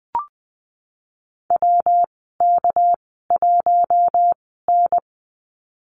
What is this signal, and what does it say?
Morse code, heard as a keyed tone, about 20 wpm, spaced as for 15 wpm, 700 Hz. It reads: WK1N
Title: 238 Lesson 38g - Letter q - Call Signs 20wpm